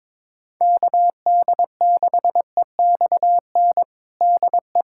Morse code, heard as a keyed tone, about 22 wpm, 700 Hz. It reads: KD6EXN DE
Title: USA Callsign DE Callsign 22wpm